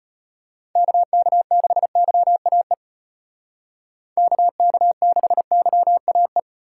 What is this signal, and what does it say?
Morse code, heard as a keyed tone, about 35 wpm, 700 Hz. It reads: KK6YAE KK6YAE